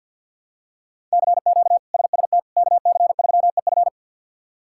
Morse code, heard as a keyed tone, about 49 wpm, 700 Hz. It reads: CQ SST KC3EF
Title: K1USN SST QSO Practice - 50wpm